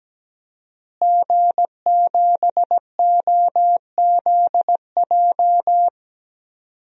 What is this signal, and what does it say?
Morse code, heard as a keyed tone, about 17 wpm, 700 Hz. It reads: G7OZJ
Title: DXCC-100 Call Signs - 5 Characters - Repeated 3x - 17wpm